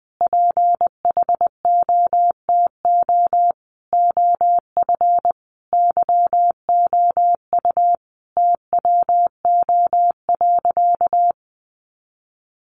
Morse code, heard as a keyed tone, about 20 wpm, 700 Hz. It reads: PHOTO OF YOU TWO.
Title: Sentences from Top 1200 Words 20wpm